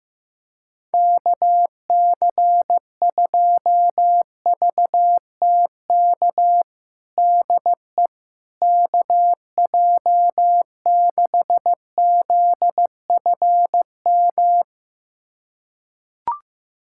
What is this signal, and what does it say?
Morse code, heard as a keyed tone, about 15 wpm, 700 Hz. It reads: KC2VTK DE KJ6ZFM
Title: USA Callsign DE Callsign 15wpm